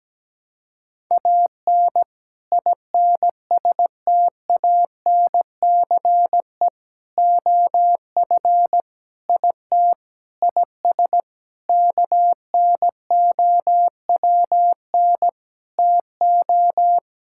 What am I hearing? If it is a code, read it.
Morse code, heard as a keyed tone, about 17 wpm, 700 Hz. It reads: AN INSTANCE OF IT IS KNOWN TO
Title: Sentences from Top 1000 Words - Review Part 5 - 17wpm